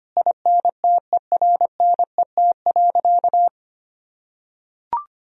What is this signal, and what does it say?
Morse code, heard as a keyed tone, about 25 wpm, 700 Hz. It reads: INTERNET.
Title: Sentences from Top 1000 Words - Review Part 4 - 25wpm